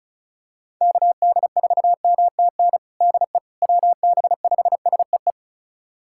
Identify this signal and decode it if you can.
Morse code, heard as a keyed tone, about 34 wpm, 700 Hz. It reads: KD4MTN DE WB5SEE
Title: USA Callsign DE Callsign - 6 Characters - 35wpm